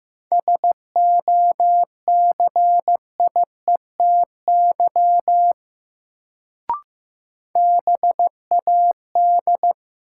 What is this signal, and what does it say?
Morse code, heard as a keyed tone, about 15 wpm, 700 Hz. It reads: SOCIETY BAD
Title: Sentences from Top 700 Words - New Word Drill - 15wpm